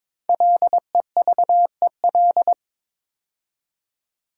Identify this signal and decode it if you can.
Morse code, heard as a keyed tone, about 22 wpm, 700 Hz. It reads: LEVEL
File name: 5 Letter Words - First 3 Letters Given - 22wpm